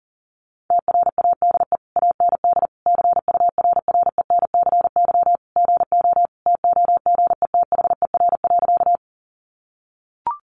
Morse code, heard as a keyed tone, about 40 wpm, 700 Hz. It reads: TRADE AND CURRENCY GO TOGETHER.
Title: Sentences from Top 1500 Words 40wpm